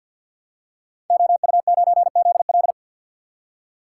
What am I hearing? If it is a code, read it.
Morse code, heard as a keyed tone, about 53 wpm, 700 Hz. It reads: OU9ZL
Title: Code Groups - 5 Letter:Numbers - 50wpm